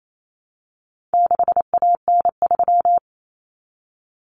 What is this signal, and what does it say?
Morse code, heard as a keyed tone, about 28 wpm, 700 Hz. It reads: 6AN3
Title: Common Cut Numbers - 4-Digit 28wpm